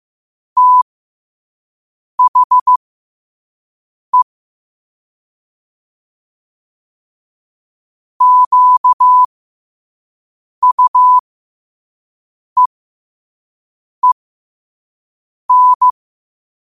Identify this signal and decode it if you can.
Morse code, heard as a keyed tone, about 15 wpm, spaced as for 5 wpm, 1000 Hz. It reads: THE QUEEN
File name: qotdfiles / 5wpm